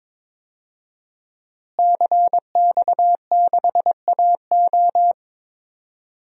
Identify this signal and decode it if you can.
Morse code, heard as a keyed tone, about 22 wpm, 700 Hz. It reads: CX6AO
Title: DXCC-100 Call Signs - 5 Characters - Repeated 3x - 22wpm